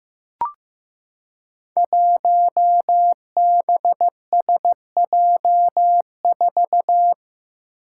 Morse code, heard as a keyed tone, about 15 wpm, 700 Hz. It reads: 1BSJ4
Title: Five Challenging Letter-Number Combinations 15wpm